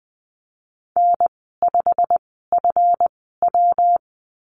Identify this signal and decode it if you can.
Morse code, heard as a keyed tone, about 20 wpm, spaced as for 15 wpm, 700 Hz. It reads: N5FW
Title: 147 Lesson 24g - Letter y - Call Signs 20wpm